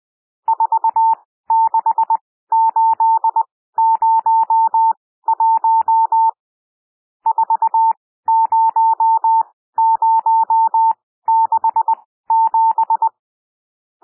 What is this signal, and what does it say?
Morse code, heard as a keyed tone, about 20 wpm, spaced as for 14 wpm, 900 Hz. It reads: 46801 40067
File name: cwmp3 / N1 SOUND NUMBERS 20 wpm